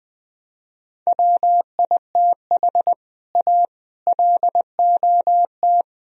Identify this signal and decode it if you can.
Morse code, heard as a keyed tone, about 20 wpm, 700 Hz. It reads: WITH A LOT
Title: Speed-Racing - Sets of 4 Words 17wpm